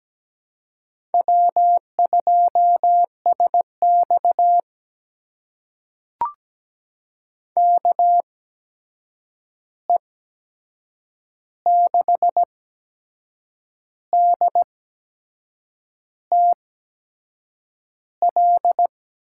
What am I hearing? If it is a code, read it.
Morse code, heard as a keyed tone, about 17 wpm, spaced as for 6 wpm, 700 Hz. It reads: W2SX KE6DTL
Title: USA Call Signs - 8x Character Spacing - 17wpm